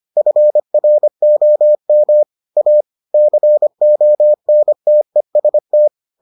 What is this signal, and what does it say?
Morse code, heard as a keyed tone, about 25 wpm, 600 Hz. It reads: FROM A CONTEST